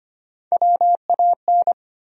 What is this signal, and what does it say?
Morse code, heard as a keyed tone, about 25 wpm, 700 Hz. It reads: WAN
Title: USA Call Sign Postfixes - 3 Letters 25wpm